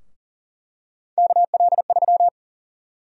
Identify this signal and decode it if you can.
Morse code, heard as a keyed tone, about 40 wpm, 700 Hz. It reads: KL3
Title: USA Call Sign Prefixes - Repeated 2x - 40wpm